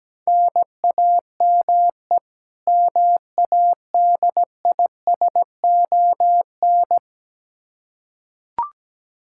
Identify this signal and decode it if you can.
Morse code, heard as a keyed tone, about 17 wpm, 700 Hz. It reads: NAME MADISON
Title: Practice QSO - Exchange 1 - Beginning - 17wpm